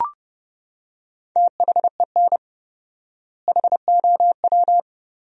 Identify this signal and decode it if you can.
Morse code, heard as a keyed tone, about 32 wpm, 700 Hz. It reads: THEN HOW
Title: 4x Word-Spacing - Sentences from Top 100 Words - 30wpm